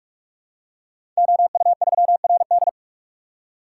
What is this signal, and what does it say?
Morse code, heard as a keyed tone, about 44 wpm, 700 Hz. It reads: OU3RD
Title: Code Groups - 5 Letter:Numbers - 45wpm